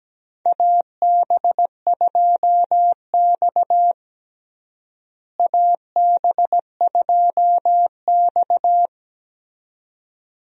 Morse code, heard as a keyed tone, about 17 wpm, 700 Hz. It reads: AB2X AB2X
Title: USA Call Signs - 4 Characters - Repeated 2x - 17wpm